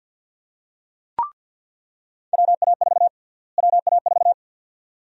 Morse code, heard as a keyed tone, about 46 wpm, 700 Hz. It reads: WA4 WA4
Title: USA Call Sign Prefixes - Repeated 2x - 50wpm